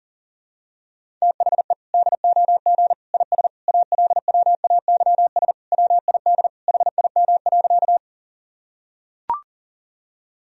Morse code, heard as a keyed tone, about 40 wpm, 700 Hz. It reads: THE DOG IS ALWAYS WID HIM.
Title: Sentences from Top 400 Words and Abbreviations 40wpm